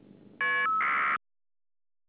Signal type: caller ID data burst